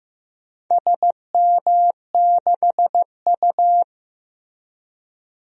Morse code, heard as a keyed tone, about 15 wpm, 700 Hz. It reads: SM6U
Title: DXCC-100 Call Signs - 4 Characters - Repeated 2x - 15wpm